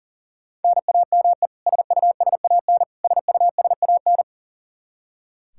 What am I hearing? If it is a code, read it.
Morse code, heard as a keyed tone, about 40 wpm, 700 Hz. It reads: NAME SUSAN SUSAN
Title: Practice QSO - Exchange 1 - Beginning - 40wpm